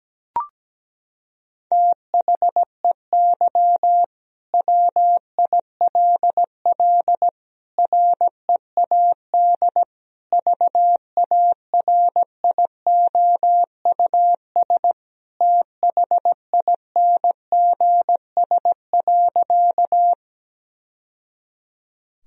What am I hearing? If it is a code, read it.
Morse code, heard as a keyed tone, about 17 wpm, 700 Hz. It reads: THEY WILL READ VARIOUS THINGS.
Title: Sentences from Top 1000 Words - Review Part 4 - 17wpm